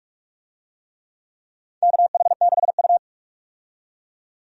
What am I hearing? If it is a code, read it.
Morse code, heard as a keyed tone, about 46 wpm, 700 Hz. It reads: KH6U
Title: DXCC-100 Call Signs - 4 Characters - 45wpm